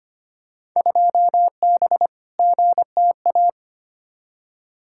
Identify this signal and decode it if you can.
Morse code, heard as a keyed tone, about 25 wpm, 700 Hz. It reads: 2B GTA
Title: Field Day Exchanges Encore - 25wpm